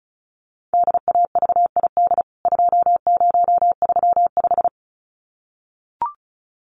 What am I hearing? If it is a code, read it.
Morse code, heard as a keyed tone, about 34 wpm, 700 Hz. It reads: DAVID 2035
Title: CWT Exchange 35wpm